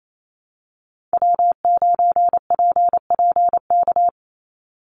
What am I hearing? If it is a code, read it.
Morse code, heard as a keyed tone, about 28 wpm, 700 Hz. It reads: W9PPK
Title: USA Call Signs 28wpm